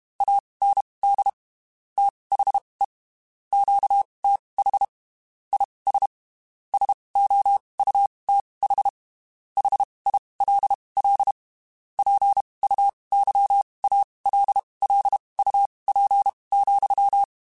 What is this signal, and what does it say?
Morse code, heard as a keyed tone, about 32 wpm, spaced as for 23 wpm, 800 Hz. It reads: AND THE QTH IS SOUTH HILL PUYALLUP,
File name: (spaced) / QSOs Spaced-mp3s / kc7haa @30 sp